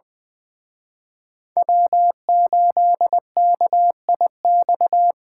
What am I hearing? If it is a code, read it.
Morse code, heard as a keyed tone, about 24 wpm, 700 Hz. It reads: W8KIX
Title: Speed Racing - USA 1x3 Call Signs 20wpm